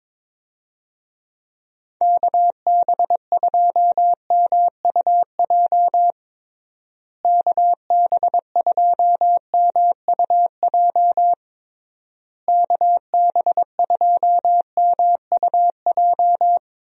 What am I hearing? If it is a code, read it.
Morse code, heard as a keyed tone, about 22 wpm, 700 Hz. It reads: KB2MUJ KB2MUJ KB2MUJ
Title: USA Call Signs - 6 Characters - Repeated 3x - 22wpm